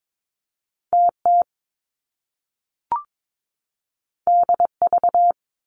Morse code, heard as a keyed tone, about 22 wpm, 700 Hz. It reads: TT DV
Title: Top Two Letter Combinations in Words 22wpm